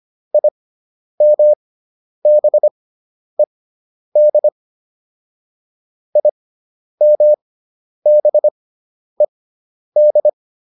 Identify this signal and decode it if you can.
Morse code, heard as a keyed tone, about 25 wpm, spaced as for 10 wpm, 600 Hz. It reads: IMBED IMBED